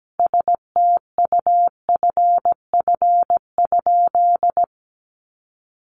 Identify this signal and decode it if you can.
Morse code, heard as a keyed tone, about 17 wpm, 700 Hz. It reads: STUFF?
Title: Sentences from Top 1400 Words 17wpm